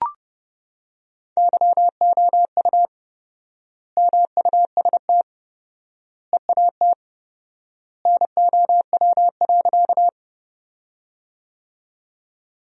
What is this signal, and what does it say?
Morse code, heard as a keyed tone, about 30 wpm, 700 Hz. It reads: YOU MUST EAT NOW.
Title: 4x Word-Spacing - Sentences from Top 300 Words - 30wpm